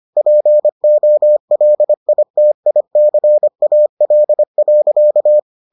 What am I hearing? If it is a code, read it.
Morse code, heard as a keyed tone, about 25 wpm, 600 Hz. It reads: POLITICAL.